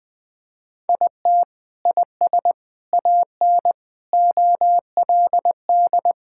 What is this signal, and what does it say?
Morse code, heard as a keyed tone, about 20 wpm, 700 Hz. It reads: IT IS AN OLD
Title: Sentences from Top 1600 Words 20wpm